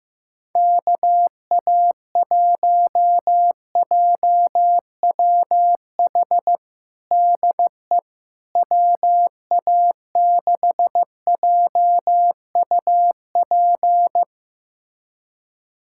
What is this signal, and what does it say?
Morse code, heard as a keyed tone, about 15 wpm, 700 Hz. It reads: KA1JWH DE WA6JUP
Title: USA Callsign DE Callsign - 6 Characters - 15wpm